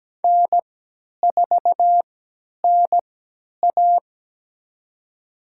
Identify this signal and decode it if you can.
Morse code, heard as a keyed tone, about 17 wpm, spaced as for 10 wpm, 700 Hz. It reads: N4NA
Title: USA Call Signs - 3x Character Spacing - 17wpm